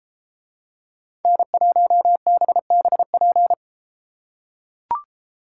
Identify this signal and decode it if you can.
Morse code, heard as a keyed tone, about 33 wpm, 700 Hz. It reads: N1BBP